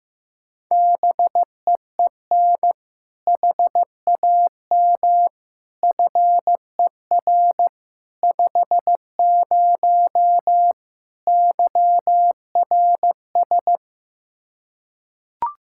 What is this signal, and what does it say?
Morse code, heard as a keyed tone, about 16 wpm, 700 Hz. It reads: BEEN HAM FER 50 YRS
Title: K1USN SST QSO Practice 15wpm